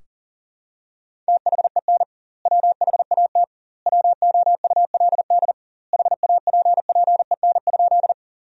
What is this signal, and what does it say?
Morse code, heard as a keyed tone, about 40 wpm, 700 Hz. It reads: THEN WHAT WOULD HAPPEN?
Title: 2x Word-Spacing - Sentences from Top 400 Words - 40wpm